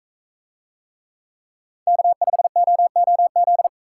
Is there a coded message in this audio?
Yes: Morse code from 1.9 to 3.7 s, heard as a keyed tone, about 42 wpm, 700 Hz. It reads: K5OOZ